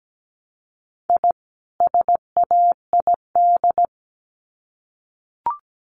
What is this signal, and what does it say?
Morse code, heard as a keyed tone, about 17 wpm, 700 Hz. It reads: I SAID
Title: Sets of 2 Words 17wpm